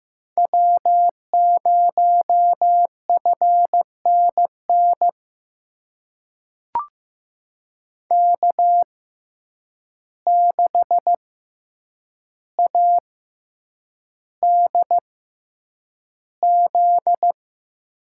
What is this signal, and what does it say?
Morse code, heard as a keyed tone, about 15 wpm, spaced as for 8 wpm, 700 Hz. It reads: W0FNN K6ADZ
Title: USA Call Signs - 5 Characters - 6x Character Spacing - 15wpm